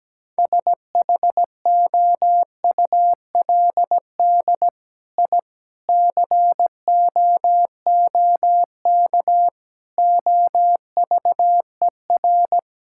Morse code, heard as a keyed tone, about 17 wpm, 700 Hz. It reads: SHOULD I COOK OVER
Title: Sentences from Top 1000 Words - Review Part 6 - 17wpm